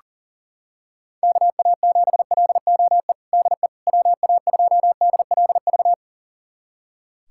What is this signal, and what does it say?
Morse code, heard as a keyed tone, about 40 wpm, 700 Hz. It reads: KA7LOE DE WA2DLV
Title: USA Callsign DE Callsign - 6 Characters - 40wpm